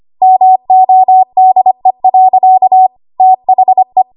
Morse code, heard as a keyed tone, about 25 wpm, 750 Hz. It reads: MODE. THE
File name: Archive / 25 WPM / 25WPM